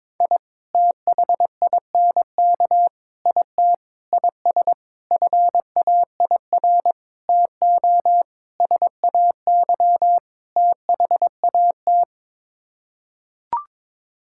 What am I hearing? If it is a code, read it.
Morse code, heard as a keyed tone, about 22 wpm, 700 Hz. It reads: I THINK IT IS FAIR TO SAY THAT
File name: practice / 22wpm / Sets of 8 Words - 22wpm